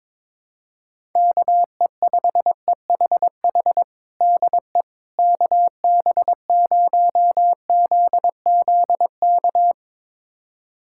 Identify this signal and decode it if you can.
Morse code, heard as a keyed tone, about 21 wpm, 700 Hz. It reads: KE5EHH DE KB0ZZK
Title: USA Callsign DE Callsign - 6 Characters - 22wpm